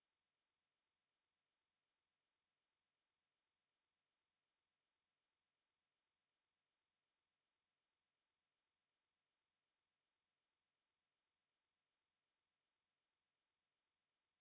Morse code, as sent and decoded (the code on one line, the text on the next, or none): none